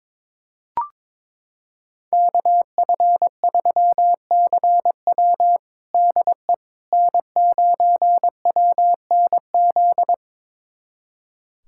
Morse code, heard as a keyed tone, about 21 wpm, 700 Hz. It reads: KF3CW DE N9WNZ
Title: USA Callsign DE Callsign - 5 Characters - 22wpm